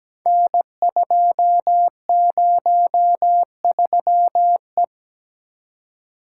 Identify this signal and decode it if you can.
Morse code, heard as a keyed tone, about 17 wpm, 700 Hz. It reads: N203E
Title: Code Groups - 5 Letter:Numbers - 17wpm